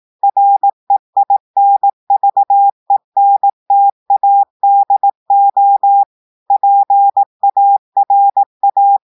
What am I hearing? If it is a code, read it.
Morse code, heard as a keyed tone, about 18 wpm, 800 Hz. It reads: REINVENTADO PARA
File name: portugais texte en clair 2 vitesse 18 mots minutes